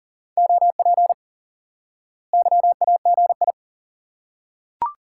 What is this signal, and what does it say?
Morse code, heard as a keyed tone, about 40 wpm, 700 Hz. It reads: OP YAGI
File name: IQE Course 3 - Lesson 89 - YAGI - Two Elements - 40wpm